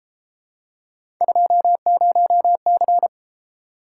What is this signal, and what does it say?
Morse code, heard as a keyed tone, about 30 wpm, 700 Hz. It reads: 20C